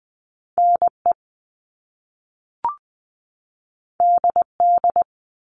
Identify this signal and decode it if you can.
Morse code, heard as a keyed tone, about 20 wpm, 700 Hz. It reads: N E D D
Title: All Cut Numbers - 2-Digit 20wpm